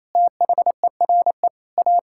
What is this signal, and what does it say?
Morse code, heard as a keyed tone, about 28 wpm, 700 Hz. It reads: THERE A
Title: Sentences from Top 1000 Words - Review Part 5 - 28wpm